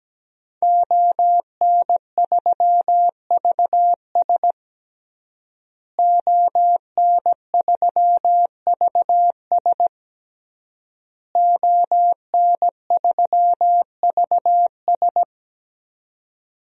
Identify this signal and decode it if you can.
Morse code, heard as a keyed tone, about 17 wpm, 700 Hz. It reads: ON3VS ON3VS ON3VS
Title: DXCC-100 Call Signs - 5 Characters - Repeated 3x - 17wpm